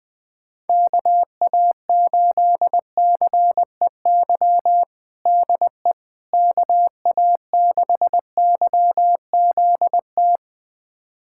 Morse code, heard as a keyed tone, about 20 wpm, 700 Hz. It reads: KA8CEY DE KA6YZT
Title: USA Callsign DE Callsign - 6 Characters - 20wpm